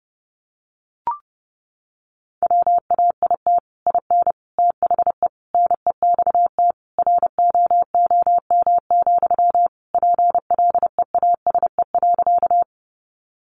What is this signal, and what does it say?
Morse code, heard as a keyed tone, about 30 wpm, 700 Hz. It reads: WAIT IN THE NEXT ROOM, PLEASE.
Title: Sentences from Top 1500 Words 30wpm